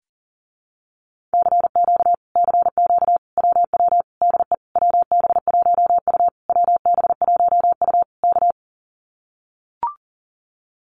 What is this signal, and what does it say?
Morse code, heard as a keyed tone, about 41 wpm, 700 Hz. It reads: CQ CQ WW DE WB1U WB1U K
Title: Common QSO Phrases Encore 40wpm